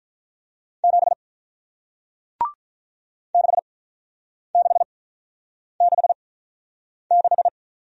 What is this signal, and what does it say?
Morse code, heard as a keyed tone, about 48 wpm, spaced as for 9 wpm, 700 Hz. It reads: 7 6666